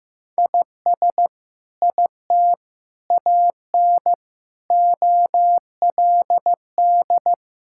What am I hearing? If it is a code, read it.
Morse code, heard as a keyed tone, about 15 wpm, 700 Hz. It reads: IS IT AN OLD
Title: Sentences from Top 1000 Words - Review Part 5 - 15wpm